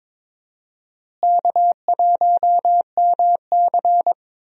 Morse code, heard as a keyed tone, about 22 wpm, 700 Hz. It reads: K1MC
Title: USA Call Signs - 4 Characters - Repeated 3x - 22wpm